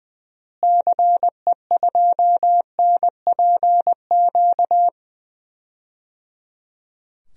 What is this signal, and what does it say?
Morse code, heard as a keyed tone, about 20 wpm, 700 Hz. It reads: CE2NPQ
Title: DXCC-100 Call Signs - 6 Characters - Repeated 3x - 20wpm